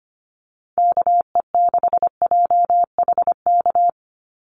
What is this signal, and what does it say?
Morse code, heard as a keyed tone, about 25 wpm, 690 Hz. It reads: KE6JHK
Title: USA Call Signs 25wpm